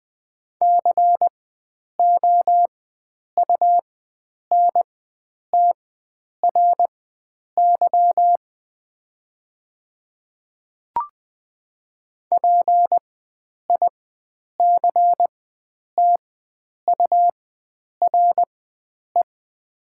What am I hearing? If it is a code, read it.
Morse code, heard as a keyed tone, about 20 wpm, spaced as for 9 wpm, 700 Hz. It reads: COUNTRY PICTURE
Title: 4x Character Spacing - Top 300 Words - 20wpm